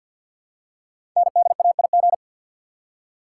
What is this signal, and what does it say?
Morse code, heard as a keyed tone, about 50 wpm, 700 Hz. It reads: NDAIG
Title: Code Groups - 5 Letters - 50wpm